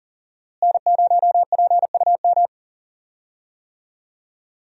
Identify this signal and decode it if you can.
Morse code, heard as a keyed tone, about 39 wpm, 700 Hz. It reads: N0PUM
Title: USA Call Signs - 5 Characters - Repeated 2x - 40wpm